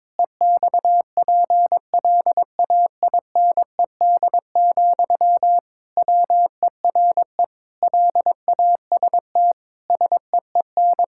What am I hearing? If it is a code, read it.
Morse code, heard as a keyed tone, about 22 wpm, 700 Hz. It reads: EXPLAINED, WERE LAST SEEN
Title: Pangram Challenge v2 - 22wpm